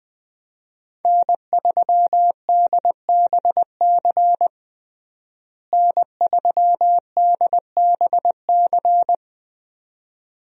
Morse code, heard as a keyed tone, about 20 wpm, 700 Hz. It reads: N3DBC N3DBC
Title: USA Call Signs - 5 Characters - Repeated 2x - 20wpm